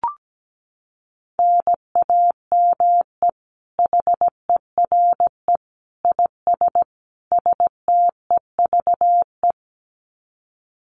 Morse code, heard as a keyed tone, about 17 wpm, 700 Hz. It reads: NAME HERE IS STEVE
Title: Common QSO Phrases 17wpm